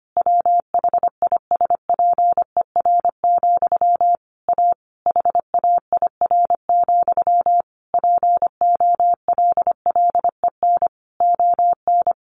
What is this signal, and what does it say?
Morse code, heard as a keyed tone, about 25 wpm, 700 Hz. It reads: WHISPER, A HAIR, POLLEN ON
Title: A Sound Of Thunder 25wpm